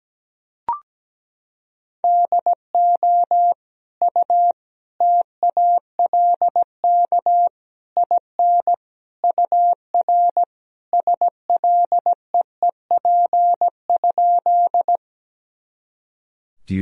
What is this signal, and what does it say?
Morse code, heard as a keyed tone, about 17 wpm, 700 Hz. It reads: DO U TALK IN UR SLEEP?
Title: Sentences from Top 500 Words and Abbreviations 17wpm